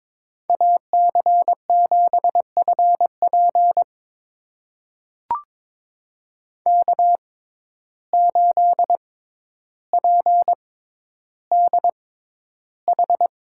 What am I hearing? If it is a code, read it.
Morse code, heard as a keyed tone, about 22 wpm, spaced as for 11 wpm, 700 Hz. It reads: AC7FP K8PDH
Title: USA Call Signs - 5 Characters - 6x Character Spacing - 22wpm